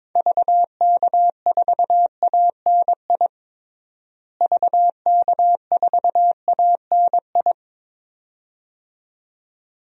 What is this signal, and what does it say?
Morse code, heard as a keyed tone, about 22 wpm, 700 Hz. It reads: VK4ANI VK4ANI
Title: DXCC-100 Call Signs - 6 Characters - Repeated 2x - 22wpm